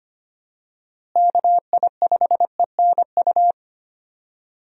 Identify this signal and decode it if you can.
Morse code, heard as a keyed tone, about 25 wpm, 700 Hz. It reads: KI5ENU
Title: USA Call Signs - 6 Characters - Repeated 2x - 25wpm